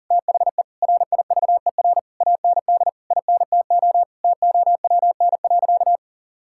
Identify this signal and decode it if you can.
Morse code, heard as a keyed tone, about 37 wpm, 700 Hz. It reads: THE RIVER AND INTO TOWN.
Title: Sentences from Top 1000 Words - Review Part 6 - 40wpm